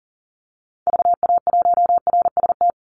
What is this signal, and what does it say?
Morse code, heard as a keyed tone, about 40 wpm, 700 Hz. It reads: VA1RST
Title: Call Signs Repeated 3x - 40wpm